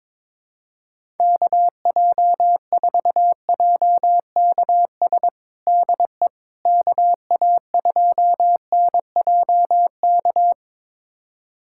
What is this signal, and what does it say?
Morse code, heard as a keyed tone, about 22 wpm, 700 Hz. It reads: KJ4JKS DE KA2NJK
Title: USA Callsign DE Callsign - 6 Characters - 22wpm